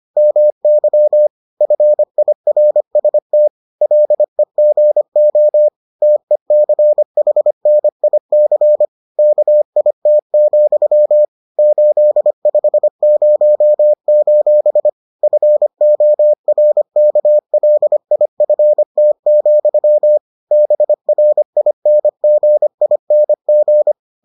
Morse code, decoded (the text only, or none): MY FIRST LEGO TECHNIC KIT, 850: FORKLIFT, BRINGING